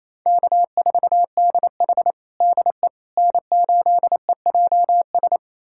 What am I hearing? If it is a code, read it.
Morse code, heard as a keyed tone, about 28 wpm, 700 Hz. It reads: K4DH DE N8EJS
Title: USA Callsign DE Callsign 28wpm